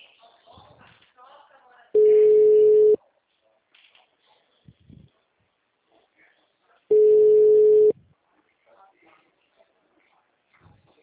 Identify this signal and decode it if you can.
Ringback tone (Europe, 425 Hz), with speech.
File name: out- -2015- - - .48